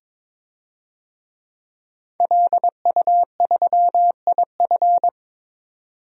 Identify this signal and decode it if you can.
Morse code, heard as a keyed tone, about 22 wpm, 700 Hz. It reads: LU3IF
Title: DXCC-100 Call Signs - 5 Characters - 22wpm